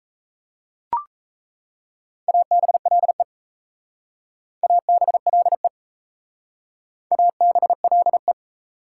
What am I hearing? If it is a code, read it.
Morse code, heard as a keyed tone, about 35 wpm, 700 Hz. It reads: ABLE ABLE ABLE
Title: Speed-Racing Prefixes and Suffixes 28wpm